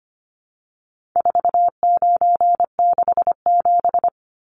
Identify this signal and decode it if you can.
Morse code, heard as a keyed tone, about 25 wpm, 700 Hz. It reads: 4967
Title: Numbers - 4-Digit 25wpm